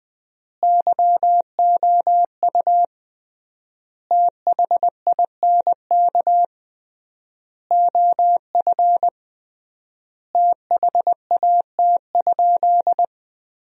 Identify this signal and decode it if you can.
Morse code, heard as a keyed tone, about 20 wpm, 700 Hz. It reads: YOU THINK OF THAT?
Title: 3x Word-Spacing - Sentences from Top 200 Words - 20wpm